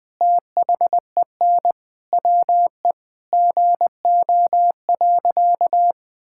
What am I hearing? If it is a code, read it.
Morse code, heard as a keyed tone, about 20 wpm, 700 Hz. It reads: THEN WE GO.
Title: Sentences with 2-Word Phrases - 20wpm